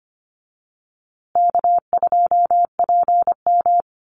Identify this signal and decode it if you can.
Morse code, heard as a keyed tone, about 25 wpm, 700 Hz. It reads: K2PM
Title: USA Call Signs Repeated 3x 25wpm